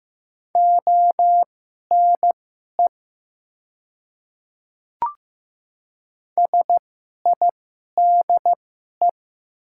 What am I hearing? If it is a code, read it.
Morse code, heard as a keyed tone, about 15 wpm, spaced as for 11 wpm, 700 Hz. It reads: ONE SIDE
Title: 2x Character Spacing - Top 100 Words - 15wpm